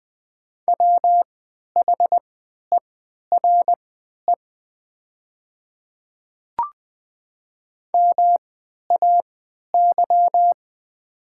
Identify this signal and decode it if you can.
Morse code, heard as a keyed tone, about 20 wpm, spaced as for 11 wpm, 700 Hz. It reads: WHERE MAY
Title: 3x Character Spacing - Top 200 Words - 20wpm